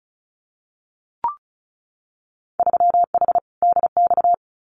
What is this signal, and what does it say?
Morse code, heard as a keyed tone, about 34 wpm, 700 Hz. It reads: 3H DX
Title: Winter Field Day Exchanges Encore 35wpm